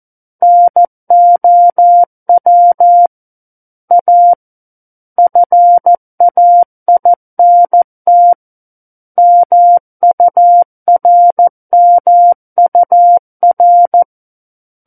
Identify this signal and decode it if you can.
Morse code, heard as a keyed tone, about 14 wpm, 700 Hz. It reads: NOW A FAINT MURMUR
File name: Book One / 05-Chapter 5 - 14 WPM